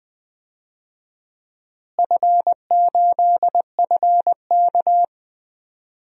Morse code, heard as a keyed tone, about 20 wpm, 700 Hz. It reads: F8FK
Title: DXCC-100 Call Signs - 4 Characters - Repeated 2x - 20wpm